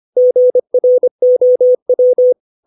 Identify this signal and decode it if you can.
Morse code, heard as a keyed tone, about 25 wpm, 500 Hz. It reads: GROW
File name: 632 words / 632 Words 25WPM